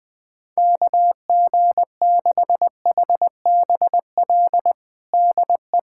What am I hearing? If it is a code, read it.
Morse code, heard as a keyed tone, about 20 wpm, 700 Hz. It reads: KG6HBL DE
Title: K1USN SST QSO Practice 20wpm